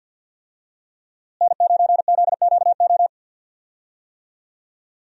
Morse code, heard as a keyed tone, about 49 wpm, 700 Hz. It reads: N9ZQO
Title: USA Call Signs - 5 Characters - Repeated 2x - 50wpm